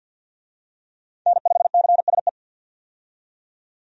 Morse code, heard as a keyed tone, about 51 wpm, 700 Hz. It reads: N5CSE
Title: USA Call Signs - 5 Characters - 50wpm